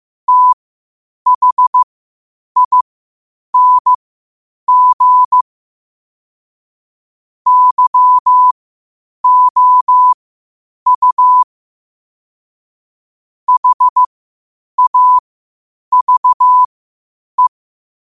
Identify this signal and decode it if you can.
Morse code, heard as a keyed tone, about 15 wpm, spaced as for 8 wpm, 1000 Hz. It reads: THING YOU HAVE
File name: qotdfiles / 8wpm